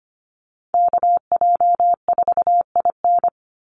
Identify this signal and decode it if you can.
Morse code, heard as a keyed tone, about 25 wpm, 700 Hz. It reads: KJ4IN
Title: USA Call Signs 25wpm